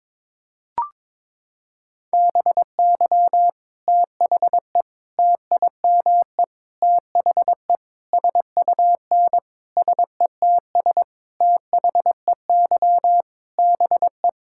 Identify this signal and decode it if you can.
Morse code, heard as a keyed tone, about 22 wpm, 700 Hz. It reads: BY THE TIME THE SUN SETS THEY BE
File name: practice / 22wpm / Sentences with 4-Word Phrases - 22wpm